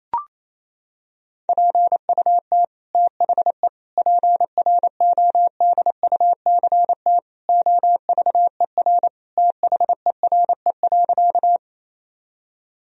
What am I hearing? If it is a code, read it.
Morse code, heard as a keyed tone, about 28 wpm, 700 Hz. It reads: PUT THE PRODUCT OVER THERE.
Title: Sentences from Top 1000 Words - Review Part 2 - 28wpm